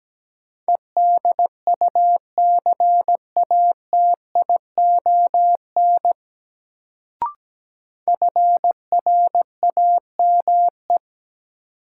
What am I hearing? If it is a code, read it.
Morse code, heard as a keyed tone, about 17 wpm, 700 Hz. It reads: EDUCATION FRAME
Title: Sentences from Top 1000 Words - Review Part 6 - New Word Drill - 17wpm